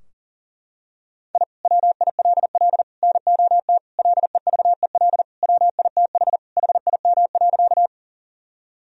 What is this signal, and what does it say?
Morse code, heard as a keyed tone, about 40 wpm, 700 Hz. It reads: I WILL NOT LEVEL WITH HIM.
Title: Sentences from Top 1000 Words - Review Part 4 - 40wpm